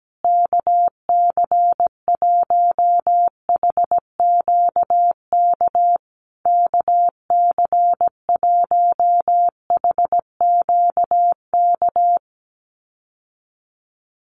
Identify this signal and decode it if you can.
Morse code, heard as a keyed tone, about 17 wpm, 700 Hz. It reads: KC1HQK KC1HQK
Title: USA Call Signs Repeated 3x 17wpm